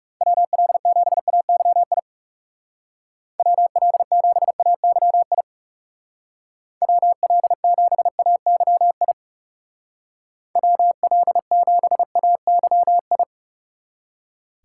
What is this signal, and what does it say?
Morse code, heard as a keyed tone, about 36 wpm, 700 Hz. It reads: WL7AYI WL7AYI WL7AYI WL7AYI
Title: Speed Racing - USA Call Signs 30wpm